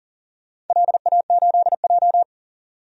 Morse code, heard as a keyed tone, about 40 wpm, 700 Hz. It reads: LA8J
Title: DXCC-100 Call Signs - 4 Characters - 40wpm